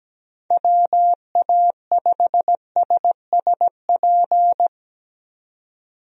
Morse code, heard as a keyed tone, about 17 wpm, 700 Hz. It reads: WA5SSP
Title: USA Call Signs - 6 Characters - 17wpm